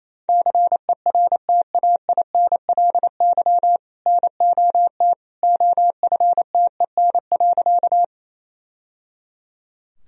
Morse code, heard as a keyed tone, about 27 wpm, 700 Hz. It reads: CERTAINLY NOT OFTEN.
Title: Sentences from Top 1000 Words - Review Part 1 - 28wpm